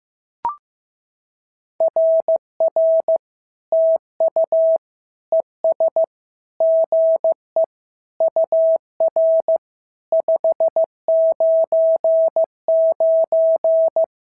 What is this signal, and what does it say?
Morse code, heard as a keyed tone, about 15 wpm, 650 Hz. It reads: RR TU ES GE UR 599
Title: SOTA QSO Practice 15wpm